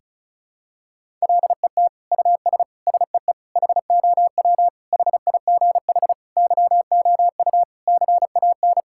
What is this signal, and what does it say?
Morse code, heard as a keyed tone, about 34 wpm, 700 Hz. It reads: LET US SEE HOW HIGH YOU CAN
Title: Sentences from Top 1000 Words - Review Part 6 - 35wpm